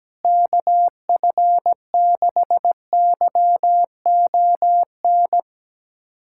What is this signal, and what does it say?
Morse code, heard as a keyed tone, about 17 wpm, 700 Hz. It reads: KF6YON
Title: USA Callsign DE Callsign 17wpm